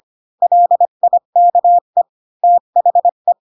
Morse code, heard as a keyed tone, about 25 wpm, 700 Hz. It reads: LIKE THE
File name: Pinocchio / Pinocchio ch3 20 25 ik1hge